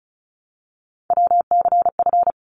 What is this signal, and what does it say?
Morse code, heard as a keyed tone, about 36 wpm, 700 Hz. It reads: WCF
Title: ARRL Sections 35wpm